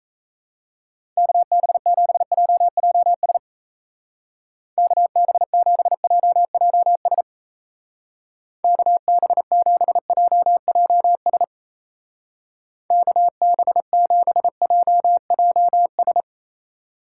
Morse code, heard as a keyed tone, about 33 wpm, 700 Hz. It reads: KB7JJS KB7JJS KB7JJS KB7JJS
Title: Speed Racing - USA 2x3 Call Signs 28wpm